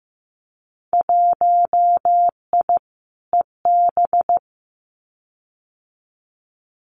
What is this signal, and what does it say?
Morse code, heard as a keyed tone, about 15 wpm, 700 Hz. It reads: 1I EB
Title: Winter Field Day Exchanges Encore 15wpm